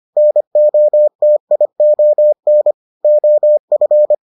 Morse code, heard as a keyed tone, about 25 wpm, 600 Hz. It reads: NOTION OF